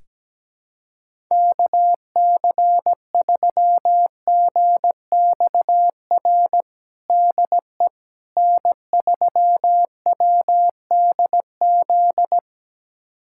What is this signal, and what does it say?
Morse code, heard as a keyed tone, about 17 wpm, 700 Hz. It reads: KC3GXR DE N3WDZ
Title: USA Callsign DE Callsign 17wpm